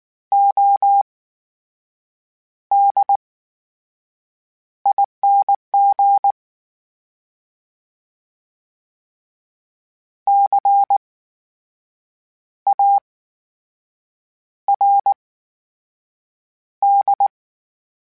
Morse code, heard as a keyed tone, about 19 wpm, spaced as for 6 wpm, 800 Hz. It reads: ODING CARD
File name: Special INGs / Special INGs-mp3s / VerySlowToFaster Ing @18